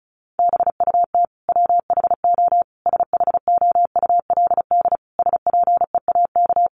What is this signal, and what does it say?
Morse code, heard as a keyed tone, about 34 wpm, 700 Hz. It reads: BUT WHO SHOULD SPEAK
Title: Sentences from Top 1100 Words 35wpm